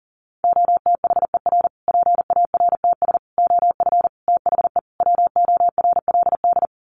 Morse code, heard as a keyed tone, about 40 wpm, 700 Hz. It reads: OTHER PARTS OF THE WORLD
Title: Sets of 5 Words Encore 40wpm